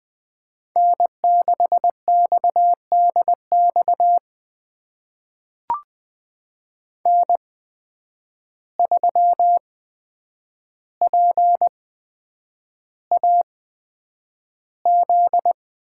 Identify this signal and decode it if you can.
Morse code, heard as a keyed tone, about 20 wpm, spaced as for 9 wpm, 700 Hz. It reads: N6XDX N3PAZ
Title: USA Call Signs - 5 Characters - 8x Character Spacing - 20wpm